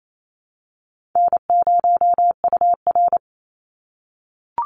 Morse code, heard as a keyed tone, about 28 wpm, 700 Hz. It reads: N0UR
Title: Easy Call Signs - 28wpm